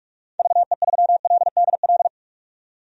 Morse code, heard as a keyed tone, about 42 wpm, 700 Hz. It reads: VE3LDL
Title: DXCC-100 Call Signs - 6 Characters - 45wpm